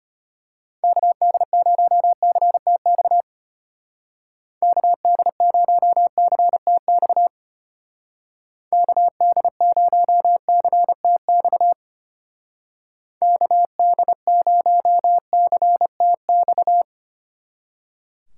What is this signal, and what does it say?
Morse code, heard as a keyed tone, about 31 wpm, 700 Hz. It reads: KD0CTX KD0CTX KD0CTX KD0CTX